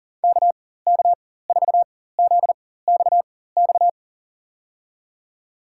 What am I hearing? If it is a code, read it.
Morse code, heard as a keyed tone, about 40 wpm, spaced as for 19 wpm, 700 Hz. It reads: KK4ZXX
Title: USA Call Signs - 4x Character Spacing - 40wpm